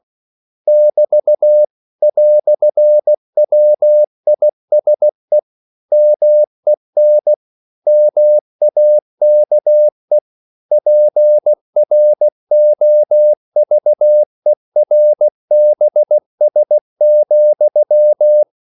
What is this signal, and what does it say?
Morse code, heard as a keyed tone, about 16 wpm, 600 Hz. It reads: <BT> "WISE MEN MAKE PROVERBS,
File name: CW / Practice / EnglishQuotations0216